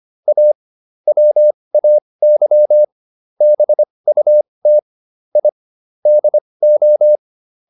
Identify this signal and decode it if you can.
Morse code, heard as a keyed tone, about 25 wpm, spaced as for 20 wpm, 600 Hz. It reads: A WAY BUT I DO